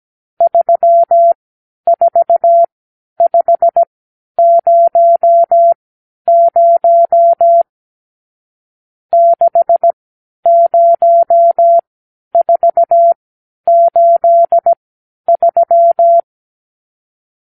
Morse code, heard as a keyed tone, about 17 wpm, spaced as for 10 wpm, 680 Hz. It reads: 34500 60483
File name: arrl / Review (1-0)